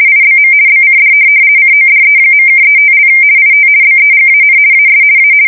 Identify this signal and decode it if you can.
RTTY, about 50 baud. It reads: IBM BCD CODE.(35) IT BEGAN AS A
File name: fsk170-50B / teletype-corp